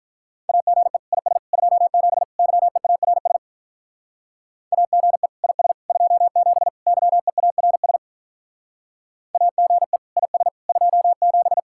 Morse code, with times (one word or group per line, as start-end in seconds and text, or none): 0.5-1.0 AGE
1.1-1.4 IS
1.5-2.2 27
2.4-3.4 YEARS
4.7-5.3 AGE
5.4-5.7 IS
5.9-6.7 27
6.9-8.0 YEARS
9.3-10.0 AGE
10.2-10.5 IS
10.7-11.6 27